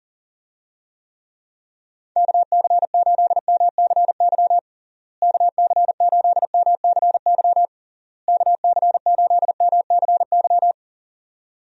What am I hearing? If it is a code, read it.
Morse code, heard as a keyed tone, about 40 wpm, 700 Hz. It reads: KC8MCY KC8MCY KC8MCY
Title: USA Call Signs - 6 Characters - Repeated 3x - 40wpm